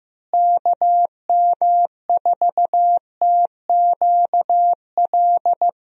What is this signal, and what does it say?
Morse code, heard as a keyed tone, about 15 wpm, 700 Hz. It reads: KM4TQL
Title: USA Call Signs - 6 Characters - Repeated 3x - 15wpm